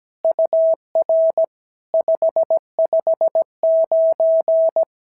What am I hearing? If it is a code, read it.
Morse code, heard as a keyed tone, about 17 wpm, 650 Hz. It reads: UR 559
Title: SOTA QSO Practice 17wpm